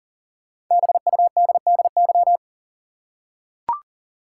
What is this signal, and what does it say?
Morse code, heard as a keyed tone, about 40 wpm, 700 Hz. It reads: BUDDY
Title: Five Letter Words - 40wpm